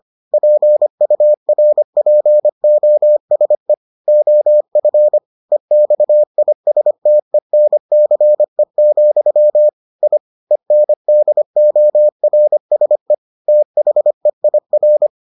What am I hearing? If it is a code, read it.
Morse code, heard as a keyed tone, about 25 wpm, 600 Hz. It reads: PURPOSE OF EXISTENCE, I ENDORSE THEIR